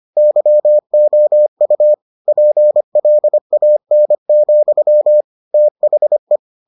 Morse code, heard as a keyed tone, about 25 wpm, 600 Hz. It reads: YOU PLAN, THE